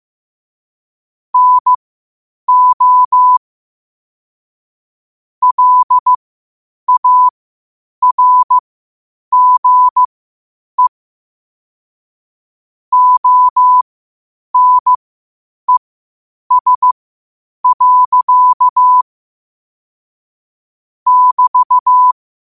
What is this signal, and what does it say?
Morse code, heard as a keyed tone, about 15 wpm, spaced as for 8 wpm, 1000 Hz. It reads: NO LARGE ONES. <BT>